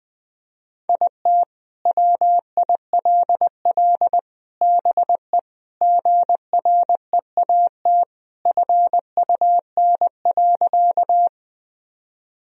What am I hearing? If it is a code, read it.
Morse code, heard as a keyed tone, about 21 wpm, 700 Hz. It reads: IT WILL BE GREAT FUN.
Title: Sentences from Top 1000 Words - Review Part 4 - 20wpm